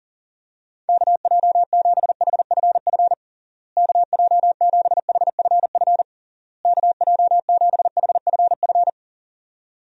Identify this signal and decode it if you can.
Morse code, heard as a keyed tone, about 40 wpm, 700 Hz. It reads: KJ7HFF KJ7HFF KJ7HFF
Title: USA Call Signs - 6 Characters - Repeated 3x - 40wpm